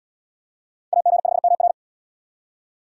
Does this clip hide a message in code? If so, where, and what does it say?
Yes: Morse code from 0.9 to 1.7 s, heard as a keyed tone, about 68 wpm, 700 Hz. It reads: AG5MR